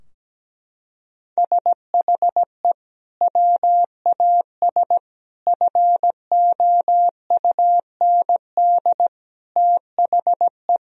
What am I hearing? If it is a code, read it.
Morse code, heard as a keyed tone, about 17 wpm, 700 Hz. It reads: SHE WAS FOUND THE
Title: Sentences from Top 1000 Words - Review Part 3 - 17wpm